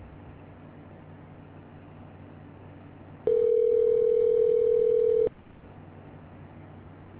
Ringback tone (North America), 440 + 480 Hz.